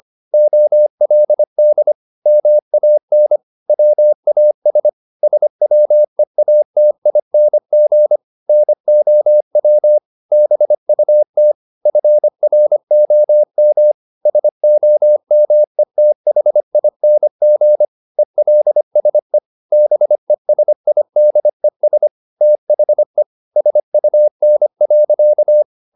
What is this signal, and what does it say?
Morse code, heard as a keyed tone, about 25 wpm, 600 Hz. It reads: OLD MAN WAS SWEATING NOW BUT FROM SOMETHING ELSE BESIDES THE SUN.